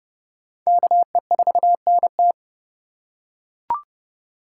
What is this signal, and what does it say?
Morse code, heard as a keyed tone, about 30 wpm, 700 Hz. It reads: KE4NT